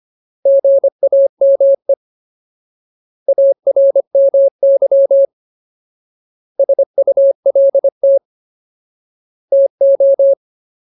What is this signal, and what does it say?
Morse code, heard as a keyed tone, about 25 wpm, 550 Hz. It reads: GAME ARMY SULT TO
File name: Ultimate Word List / 25WPM Ultimate